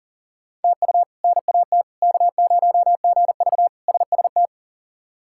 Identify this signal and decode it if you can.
Morse code, heard as a keyed tone, about 40 wpm, 700 Hz. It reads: TU NAT K0GV SST
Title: K1USN SST QSO Practice - 40wpm